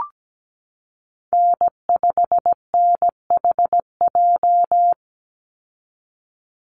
Morse code, heard as a keyed tone, about 17 wpm, 700 Hz. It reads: N5NHJ
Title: Call Signs - 17wpm